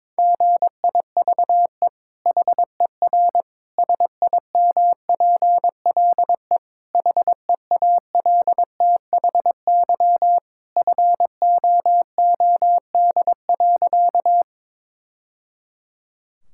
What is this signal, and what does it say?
Morse code, heard as a keyed tone, about 22 wpm, 700 Hz. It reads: GIVE HER SIMPLE HEALTHY FOOD.
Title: Sentences from Top 1000 Words - Review Part 5 - 22wpm